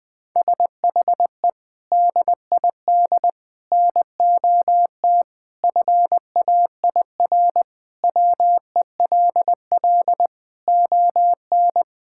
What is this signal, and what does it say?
Morse code, heard as a keyed tone, about 20 wpm, 700 Hz. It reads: SHE DID NOT FAIR WELL ON
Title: Sentences from Top 1000 Words - Review Part 5 - 20wpm